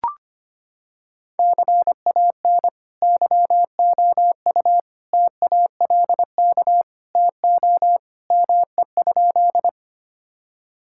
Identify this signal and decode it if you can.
Morse code, heard as a keyed tone, about 25 wpm, 700 Hz. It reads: CAN YOU TALK TO ME?
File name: practice / 25wpm / Sentences with 2-Word Phrases - 25wpm